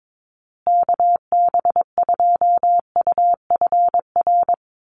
Morse code, heard as a keyed tone, about 22 wpm, 700 Hz. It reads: KB2UFR
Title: USA Call Signs Repeated 3x 22wpm